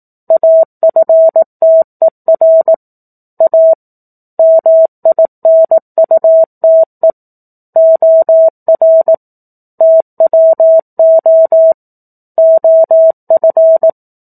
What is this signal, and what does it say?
Morse code, heard as a keyed tone, about 18 wpm, 650 Hz. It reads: AFTER A MINUTE OR TWO OF
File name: 04-Chapter-04 18 WPM